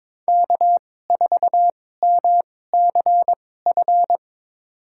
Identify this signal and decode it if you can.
Morse code, heard as a keyed tone, about 22 wpm, spaced as for 16 wpm, 700 Hz. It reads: K4MCF
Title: USA Call Signs - 5 Characters - 2x Character Spacing - 22wpm